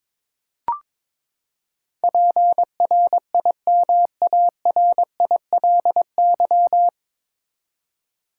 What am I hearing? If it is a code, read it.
Morse code, heard as a keyed tone, about 22 wpm, 700 Hz. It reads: PRIMARILY
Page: Morse Code Ninja practice 22wpm